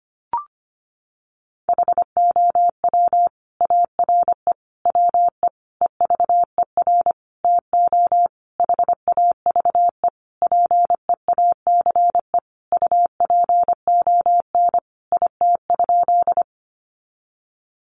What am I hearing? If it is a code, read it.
Morse code, heard as a keyed tone, about 25 wpm, 700 Hz. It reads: HOW ARE WE EVER TO HAVE PEACE UPON IT?
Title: Sentences from Top 1500 Words 25wpm